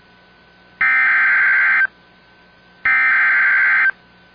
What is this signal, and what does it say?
Emergency Alert System (SAME) tones (header bursts), heard as harsh buzzing bursts. It is heard from 0.8 to 3.9 s.